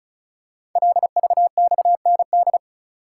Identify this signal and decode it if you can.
Morse code, heard as a keyed tone, about 36 wpm, 690 Hz. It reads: LVXND
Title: Code Groups - 5 Letters - 35wpm